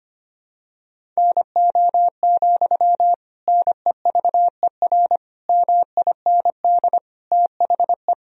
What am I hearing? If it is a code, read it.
Morse code, heard as a keyed tone, about 25 wpm, 700 Hz. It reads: NO, NEVER MIND THE
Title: Sentences from Top 1000 Words - Review Part 1 - 25wpm